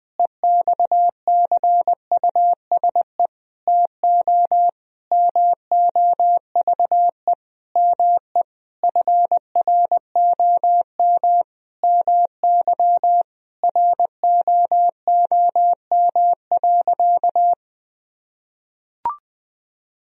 Morse code, heard as a keyed tone, about 20 wpm, 700 Hz. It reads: EXCUSE TO MOVE ME FROM MY ROOM.
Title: Sentences from Top 1000 Words - Review Part 1 - 20wpm